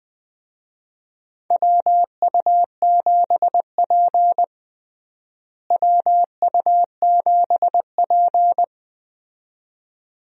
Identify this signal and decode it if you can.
Morse code, heard as a keyed tone, about 20 wpm, 700 Hz. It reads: WU7P WU7P
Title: USA Call Signs - 4 Characters - Repeated 2x - 20wpm